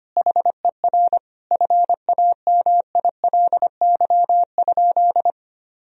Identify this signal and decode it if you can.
Morse code, heard as a keyed tone, about 25 wpm, 700 Hz. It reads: HER FAMILY?
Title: Sentences from Top 1000 Words - Review Part 3 - 25wpm